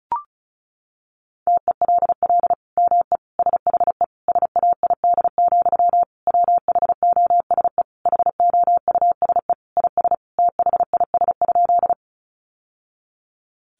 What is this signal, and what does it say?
Morse code, heard as a keyed tone, about 34 wpm, 700 Hz. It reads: TELL ME SHE SAID, WHOSE HOUSE IS THIS?
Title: Sentences from Top 1200 Words 35wpm